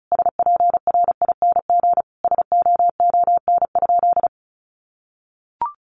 Morse code, heard as a keyed tone, about 34 wpm, 700 Hz. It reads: SPRING SOON?
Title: Sentences from Top 1200 Words 35wpm